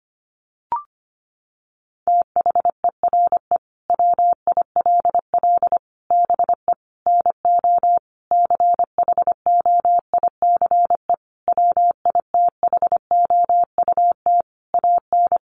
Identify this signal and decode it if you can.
Morse code, heard as a keyed tone, about 25 wpm, 700 Hz. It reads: THERE WILL BE NO CHOICE WITHOUT AN
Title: Sentences from Top 1500 Words 25wpm